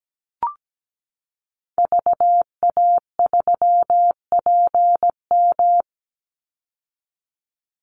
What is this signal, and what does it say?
Morse code, heard as a keyed tone, about 17 wpm, 700 Hz. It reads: VA3PM
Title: Call Signs - 17wpm